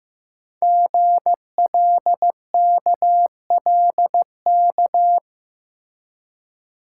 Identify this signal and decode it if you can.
Morse code, heard as a keyed tone, about 15 wpm, 690 Hz. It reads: GLKLK
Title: Code Groups - 5 Letters - 15wpm